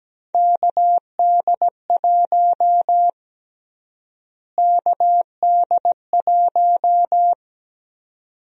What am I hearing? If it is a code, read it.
Morse code, heard as a keyed tone, about 17 wpm, 700 Hz. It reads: KD1 KD1
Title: USA Call Sign Prefixes - Repeated 2x - 17wpm